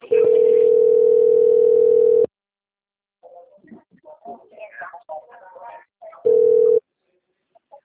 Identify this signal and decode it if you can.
Ringback tone (North America), 440 + 480 Hz.